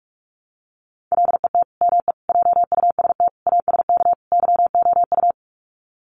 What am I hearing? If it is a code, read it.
Morse code, heard as a keyed tone, about 44 wpm, 700 Hz. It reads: LET ME JUST ASK YOU
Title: Sets of 5 Words Encore2 45wpm